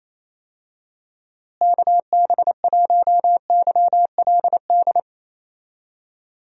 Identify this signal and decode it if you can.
Morse code, heard as a keyed tone, about 28 wpm, 700 Hz. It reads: KB1YLD
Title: USA Call Signs - 6 Characters - Repeated 2x - 28wpm